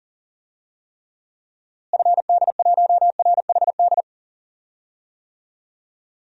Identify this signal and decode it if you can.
Morse code, heard as a keyed tone, about 37 wpm, 700 Hz. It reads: FD1RHD
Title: DXCC-100 Call Signs - 6 Characters - Repeated 2x - 40wpm